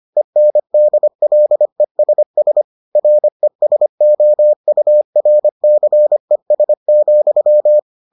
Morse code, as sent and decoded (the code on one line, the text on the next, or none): . -. -.. .-.. ... ... / .-. ... --- ..- .-. -.-. ... --..--
ENDLESS RESOURCES,